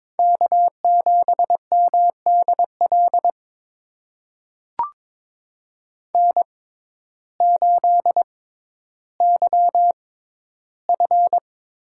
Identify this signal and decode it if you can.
Morse code, heard as a keyed tone, about 22 wpm, spaced as for 12 wpm, 700 Hz. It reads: K7MDL N8YF
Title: USA Call Signs - 5 Characters - 6x Character Spacing - 22wpm